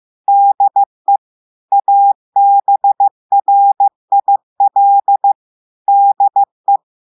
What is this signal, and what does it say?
Morse code, heard as a keyed tone, about 15 wpm, 800 Hz. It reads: DE ABRIL DE
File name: eh1ard-amateur-radio-day-2016 mf feed 1